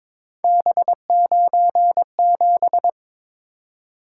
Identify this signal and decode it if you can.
Morse code, heard as a keyed tone, about 22 wpm, 700 Hz. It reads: B97
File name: practice / 22wpm / Three Letter-Number Combinations - 22wpm